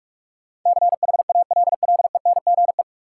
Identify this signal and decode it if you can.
Morse code, heard as a keyed tone, about 44 wpm, 700 Hz. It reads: CHALLENGE
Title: Sentences from Top 1000 Words - Review Part 7 - New Word Drill - 45wpm